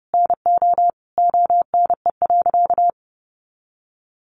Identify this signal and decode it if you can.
Morse code, heard as a keyed tone, about 30 wpm, 700 Hz. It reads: NO ONE.
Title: Additional Easy Phrases 30wpm